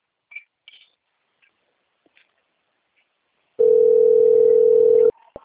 Ringback tone (North America), 440 + 480 Hz.